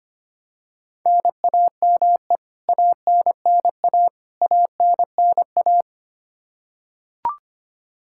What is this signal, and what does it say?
Morse code, heard as a keyed tone, about 25 wpm, 700 Hz. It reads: NAME ANNA ANNA
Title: Practice QSO - Exchange 1 - Beginning - 25wpm